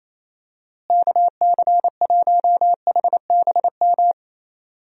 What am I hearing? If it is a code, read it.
Morse code, heard as a keyed tone, about 27 wpm, 700 Hz. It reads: KC1HBM
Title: USA Call Signs - 6 Characters - Repeated 3x - 28wpm